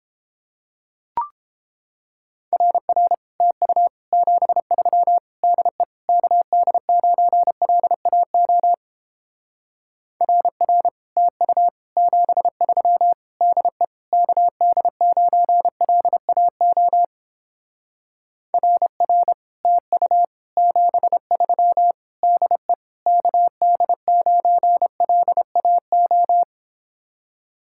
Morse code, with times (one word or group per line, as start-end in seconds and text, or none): 2.5-3.1 RR
3.4-3.9 TU
4.1-5.2 73
5.4-5.8 DE
6.1-8.7 KD9LAO
10.2-10.9 RR
11.2-11.7 TU
12.0-13.1 73
13.4-13.8 DE
14.1-17.0 KD9LAO
18.5-19.3 RR
19.7-20.2 TU
20.6-21.9 73
22.2-22.7 DE
23.1-26.4 KD9LAO